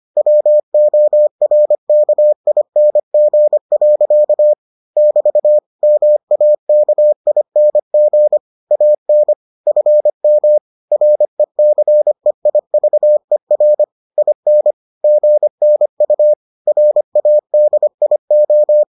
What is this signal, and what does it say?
Morse code, heard as a keyed tone, about 25 wpm, 600 Hz. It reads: WORKING. <BT> MAKING AN FM RECEIVER IN GNU RADIO